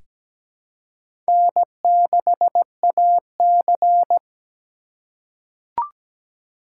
Morse code, heard as a keyed tone, about 17 wpm, 700 Hz. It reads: N6AC
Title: USA Call Signs - 8x Character Spacing - 17wpm